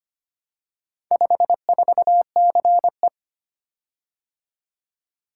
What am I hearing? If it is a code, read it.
Morse code, heard as a keyed tone, about 25 wpm, 700 Hz. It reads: 54CE
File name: practice / 25wpm / Four Letter-Number Combinations - 25wpm